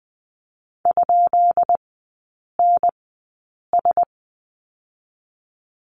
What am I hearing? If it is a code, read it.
Morse code, heard as a keyed tone, about 20 wpm, spaced as for 8 wpm, 690 Hz. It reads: ?NS
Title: 159 Lesson 26e - Letter g - Three Characters 20wpm